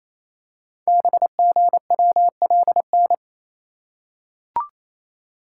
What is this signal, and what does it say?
Morse code, heard as a keyed tone, about 27 wpm, 700 Hz. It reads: BGWLN
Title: Code Groups - 5 Letters - 28wpm